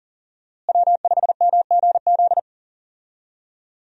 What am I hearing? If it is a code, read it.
Morse code, heard as a keyed tone, about 40 wpm, 700 Hz. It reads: W5MGZ
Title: USA Call Signs - 5 Characters - Repeated 2x - 40wpm